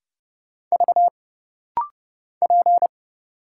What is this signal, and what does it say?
Morse code, heard as a keyed tone, about 30 wpm, 700 Hz. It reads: V P
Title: Single Letter-Number Alt - Rapid-Fire 30wpm